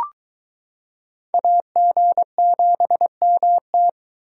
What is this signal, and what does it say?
Morse code, heard as a keyed tone, about 22 wpm, 700 Hz. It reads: AG7MT